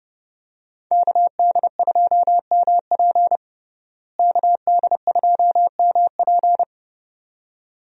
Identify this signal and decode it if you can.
Morse code, heard as a keyed tone, about 30 wpm, 700 Hz. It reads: KD2MP KD2MP
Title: USA Call Signs - 5 Characters - Repeated 2x - 30wpm